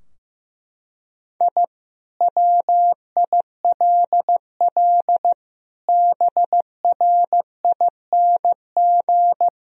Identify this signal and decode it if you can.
Morse code, heard as a keyed tone, about 15 wpm, 700 Hz. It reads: I WILL BRING
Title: Sentences from Top 1000 Words - Review Part 3 - 15wpm